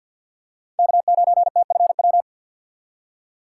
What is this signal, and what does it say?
Morse code, heard as a keyed tone, about 53 wpm, 690 Hz. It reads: K9TFW
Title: USA Call Signs - 5 Characters - 2x Character Spacing - 50wpm